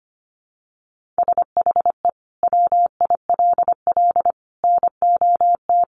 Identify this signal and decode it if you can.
Morse code, heard as a keyed tone, about 25 wpm, 700 Hz. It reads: SHE WILL NOT
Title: Sentences from Top 1500 Words 25wpm